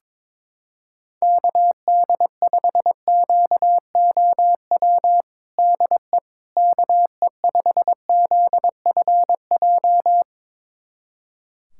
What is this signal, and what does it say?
Morse code, heard as a keyed tone, about 22 wpm, 700 Hz. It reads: KD5QOW DE KE5ZFJ
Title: USA Callsign DE Callsign 22wpm